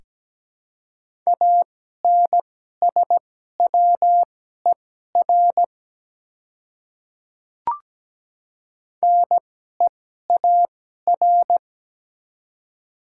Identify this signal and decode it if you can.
Morse code, heard as a keyed tone, about 17 wpm, spaced as for 12 wpm, 700 Hz. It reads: ANSWER NEAR
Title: 2x Character Spacing - Top 300 Words - 17wpm